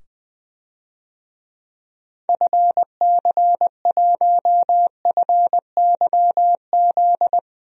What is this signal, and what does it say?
Morse code, heard as a keyed tone, about 20 wpm, 700 Hz. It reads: FC1FYZ
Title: DXCC-100 Call Signs - 6 Characters - 20wpm